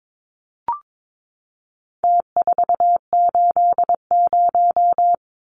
Morse code, heard as a keyed tone, about 22 wpm, 700 Hz. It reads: T480
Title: Numbers - Contest Style - 4-Digit 22wpm